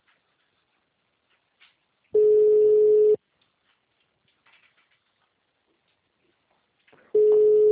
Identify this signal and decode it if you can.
Ringback tone (Europe, 425 Hz).
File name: out- -2008